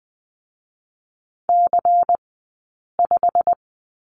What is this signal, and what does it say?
Morse code, heard as a keyed tone, about 20 wpm, spaced as for 8 wpm, 700 Hz. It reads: C5
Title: 144 Lesson 24d - Letter y - Two Characters 20wpm